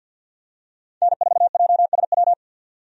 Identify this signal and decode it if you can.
Morse code, heard as a keyed tone, about 48 wpm, 700 Hz. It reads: N4JSW
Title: USA Call Signs - 5 Characters - 50wpm